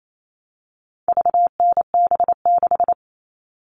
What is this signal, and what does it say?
Morse code, heard as a keyed tone, about 28 wpm, 700 Hz. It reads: VNB6
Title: All Cut Numbers - 4-Digit 28wpm